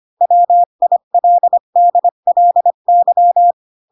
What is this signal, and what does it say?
Morse code, heard as a keyed tone, about 25 wpm, 700 Hz. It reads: WILDLY
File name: Pinocchio / Pinocchio ch4 20 25 ik1hge